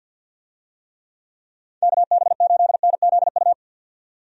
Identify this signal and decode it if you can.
Morse code, heard as a keyed tone, about 47 wpm, 700 Hz. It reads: KB8NZU
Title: USA Call Signs - 6 Characters - 50wpm